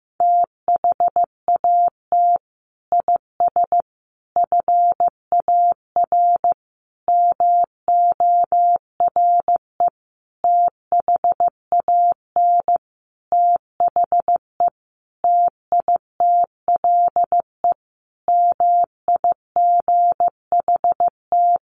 Morse code, heard as a keyed tone, about 15 wpm, 700 Hz. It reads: THAT IS FAR MORE THAN THE TITLE MIGHT
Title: Sentences from Top 1200 Words 15wpm